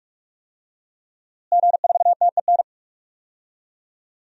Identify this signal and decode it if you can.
Morse code, heard as a keyed tone, about 46 wpm, 700 Hz. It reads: G4TEN
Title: DXCC-100 Call Signs - 5 Characters - Repeated 2x - 45wpm